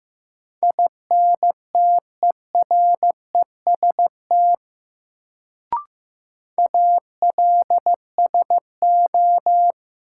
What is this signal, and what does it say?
Morse code, heard as a keyed tone, about 15 wpm, 700 Hz. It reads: INTEREST ALSO
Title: Sentences from Top 400 Words - New Word Drill - 15wpm